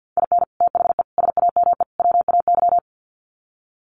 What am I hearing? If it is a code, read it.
Morse code, heard as a keyed tone, about 48 wpm, 700 Hz. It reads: IN THE SAME WAY
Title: Sets of 4 Words 50wpm